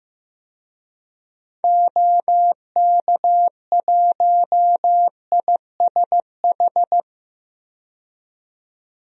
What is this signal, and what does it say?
Morse code, heard as a keyed tone, about 16 wpm, 700 Hz. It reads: OK1ISH
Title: DXCC-100 Call Signs - 6 Characters - 15wpm